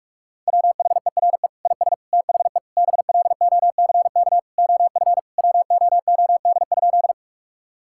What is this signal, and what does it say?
Morse code, heard as a keyed tone, about 45 wpm, 700 Hz. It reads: WHERE IS THE BLOCK OF WOOD?
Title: Sentences from Top 1600 Words 45wpm